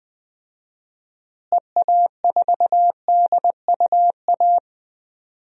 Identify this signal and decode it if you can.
Morse code, heard as a keyed tone, about 21 wpm, 700 Hz. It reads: EA4DUA
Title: DXCC-100 Call Signs - 6 Characters - Repeated 3x - 20wpm